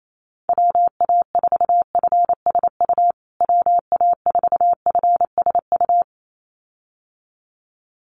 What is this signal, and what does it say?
Morse code, heard as a keyed tone, about 28 wpm, 700 Hz. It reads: WA4FSU WA4FSU
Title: USA Call Signs Repeated 2x 28wpm